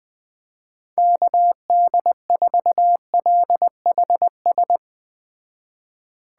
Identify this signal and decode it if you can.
Morse code, heard as a keyed tone, about 20 wpm, 700 Hz. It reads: KD4LHS
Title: Speed Racing - USA 2x3 Call Signs 17wpm